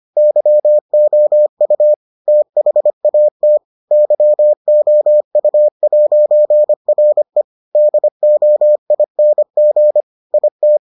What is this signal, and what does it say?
Morse code, heard as a keyed tone, about 25 wpm, 600 Hz. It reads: YOU THAT YOU'RE DOING IT